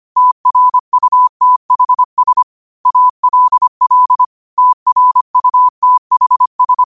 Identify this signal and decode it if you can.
Morse code, heard as a keyed tone, about 25 wpm, 1000 Hz. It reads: TRUTHS ALL TRUTHS
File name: qotdfiles / 25wpm